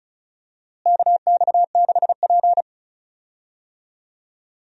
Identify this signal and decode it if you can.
Morse code, heard as a keyed tone, about 34 wpm, 700 Hz. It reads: KX6P
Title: USA Call Signs - 4 Characters - Repeated 3x - 35wpm